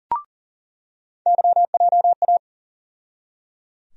Morse code, heard as a keyed tone, about 37 wpm, 700 Hz. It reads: YJA
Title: USA Call Sign Postfixes - 3 Letters 40wpm